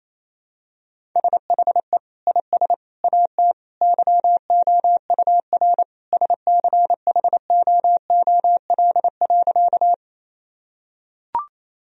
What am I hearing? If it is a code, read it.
Morse code, heard as a keyed tone, about 28 wpm, 700 Hz. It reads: SHE IS AT YOUR SCHOOL.
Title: Sentences from Top 1000 Words - Review Part 1 - 28wpm